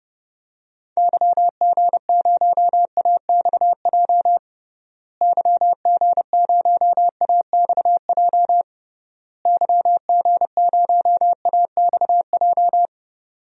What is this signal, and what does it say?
Morse code, heard as a keyed tone, about 30 wpm, 700 Hz. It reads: YG0AXJ YG0AXJ YG0AXJ
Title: DXCC-100 Call Signs - 6 Characters - Repeated 3x - 30wpm